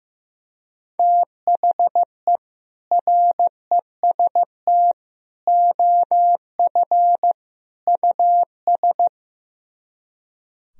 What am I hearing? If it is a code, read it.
Morse code, heard as a keyed tone, about 15 wpm, 700 Hz. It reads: THE REST OF US
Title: Speed-Racing - Sets of 4 Words 15wpm